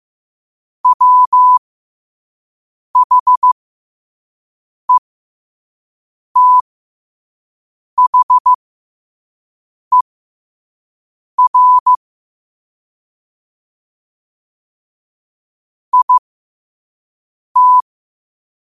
Morse code, heard as a keyed tone, about 15 wpm, spaced as for 5 wpm, 1000 Hz. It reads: WHETHER IT